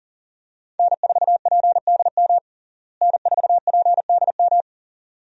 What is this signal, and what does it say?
Morse code, heard as a keyed tone, about 40 wpm, 700 Hz. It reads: N4PDM N4PDM
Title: USA Call Signs - 5 Characters - Repeated 2x - 40wpm